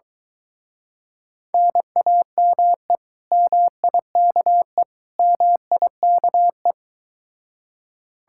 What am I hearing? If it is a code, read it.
Morse code, heard as a keyed tone, about 23 wpm, 700 Hz. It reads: NAME MIKE MIKE